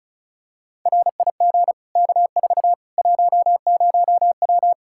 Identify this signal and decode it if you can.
Morse code, heard as a keyed tone, about 34 wpm, 700 Hz. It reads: RIG K4 10W
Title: Practice QSO - Exchange 2 - Info - 35wpm